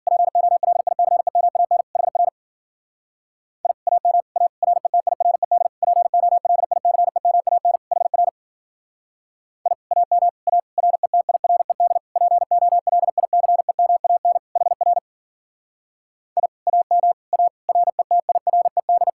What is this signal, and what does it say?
Morse code, heard as a keyed tone, about 48 wpm, 700 Hz. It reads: POLICEMAN HR I AM A RETIRED POLICEMAN HR I AM A RETIRED POLICEMAN HR I AM A RETIRED
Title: Speed-Racing Common QSO Phrases - 40wpm